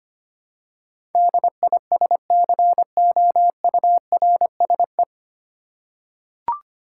Morse code, heard as a keyed tone, about 25 wpm, 700 Hz. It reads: DISCOURSE
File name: Nine Letter Words - 25wpm